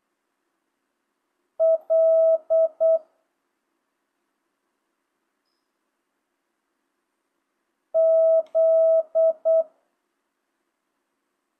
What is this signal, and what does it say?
Morse code, heard as a keyed tone, about 8 wpm, 650 Hz. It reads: L Z